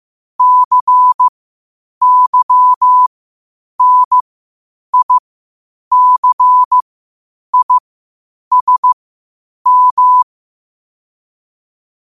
Morse code, heard as a keyed tone, about 15 wpm, spaced as for 8 wpm, 1000 Hz. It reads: CYNICISM
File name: 8wpm